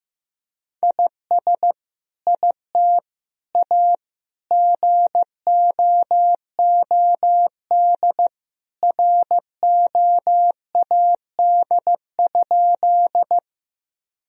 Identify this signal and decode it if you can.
Morse code, heard as a keyed tone, about 16 wpm, 700 Hz. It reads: IS IT A GOOD ROAD?
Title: Sentences from Top 1000 Words - Review Part 2 - 15wpm